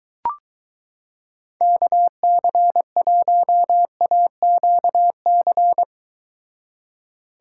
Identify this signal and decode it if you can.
Morse code, heard as a keyed tone, about 23 wpm, 700 Hz. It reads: KC1AQC